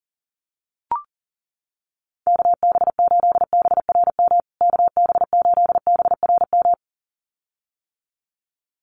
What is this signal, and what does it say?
Morse code, heard as a keyed tone, about 40 wpm, 700 Hz. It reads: KB8BRM KB8BRM
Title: USA Call Signs Repeated 2x 40wpm